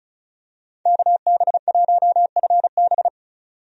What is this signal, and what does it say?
Morse code, heard as a keyed tone, about 36 wpm, 700 Hz. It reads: KB1FB
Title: USA Call Signs - 5 Characters - 35wpm